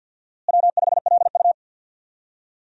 Morse code, heard as a keyed tone, about 49 wpm, 700 Hz. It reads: W5LU
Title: USA Call Signs - 4 Characters - Repeated 3x - 50wpm